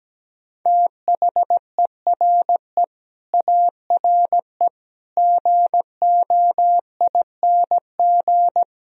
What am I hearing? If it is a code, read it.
Morse code, heard as a keyed tone, about 17 wpm, 700 Hz. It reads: THERE ARE GOING
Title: Speed-Racing - Sets of 5 Words 17wpm